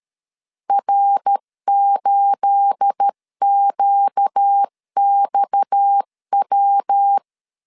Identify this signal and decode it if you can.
Morse code, heard as a keyed tone, about 13 wpm, 800 Hz. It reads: R8QXW
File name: LEZIONI / 9 28 12wpm 2351